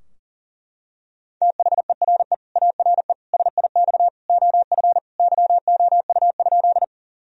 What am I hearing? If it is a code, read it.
Morse code, heard as a keyed tone, about 40 wpm, 700 Hz. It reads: THERE ARE SIX OF YOU?
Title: Sentences from Top 1000 Words - Review Part 2 - 40wpm